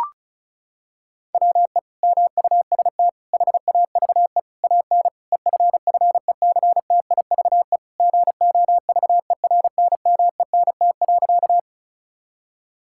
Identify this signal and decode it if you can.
Morse code, heard as a keyed tone, about 36 wpm, 700 Hz. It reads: WE MUST HAVE AN EFFECTIVE GOVERNMENT.
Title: Sentences from Top 1000 Words - Review Part 6 - 35wpm